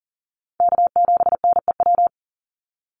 Morse code, heard as a keyed tone, about 40 wpm, 700 Hz. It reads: K7NEW
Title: NA SOTA Chaser Callsigns 40wpm